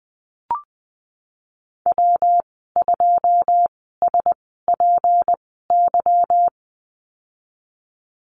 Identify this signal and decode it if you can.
Morse code, heard as a keyed tone, about 20 wpm, spaced as for 15 wpm, 700 Hz. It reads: W2SPY
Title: 179 Lesson 29g - Slash - Call Signs 20wpm